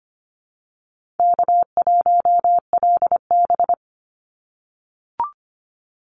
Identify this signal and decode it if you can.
Morse code, heard as a keyed tone, about 25 wpm, 700 Hz. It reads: K1LB
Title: NA SOTA Activator Callsigns 25wpm